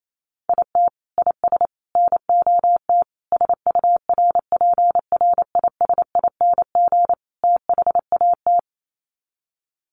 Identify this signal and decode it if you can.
Morse code, heard as a keyed tone, about 28 wpm, 700 Hz. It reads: IT IS NOT SURPRISING THAT
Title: Sets of 5 Words 28wpm